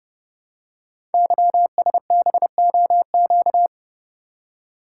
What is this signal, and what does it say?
Morse code, heard as a keyed tone, about 30 wpm, 700 Hz. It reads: YSBOQ
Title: Code Groups - 5 Letters - 30wpm